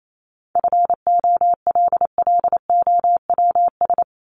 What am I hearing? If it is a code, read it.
Morse code, heard as a keyed tone, about 28 wpm, 700 Hz. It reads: FOLLOWS
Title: ES:S Suffix - Top 100 Words with Base Words 28wpm